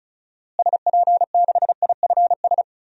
Morse code, heard as a keyed tone, about 36 wpm, 700 Hz. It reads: SP6IFS
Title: DXCC-100 Call Signs - 6 Characters - Repeated 2x - 35wpm